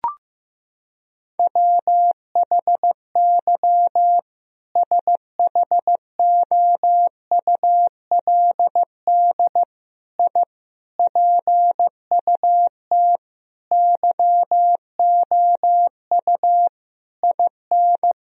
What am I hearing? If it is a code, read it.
Morse code, heard as a keyed tone, about 15 wpm, 700 Hz. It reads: WHY SHOULD I PUT YOU IN
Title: Sentences from Top 1000 Words - Review Part 4 - 15wpm